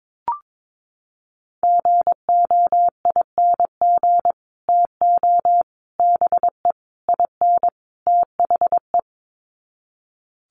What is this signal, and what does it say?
Morse code, heard as a keyed tone, about 22 wpm, 700 Hz. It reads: GOING TO BE IN THE
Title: Sets of 5 Words Encore 22wpm